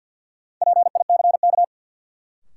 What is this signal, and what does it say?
Morse code, heard as a keyed tone, about 52 wpm, 700 Hz. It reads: PICK
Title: Sentences from Top 1000 Words - New Word Drill - 50wpm